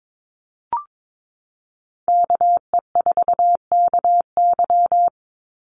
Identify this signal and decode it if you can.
Morse code, heard as a keyed tone, about 22 wpm, 690 Hz. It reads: KE4KY
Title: Call Signs Repeated 2x - 22wpm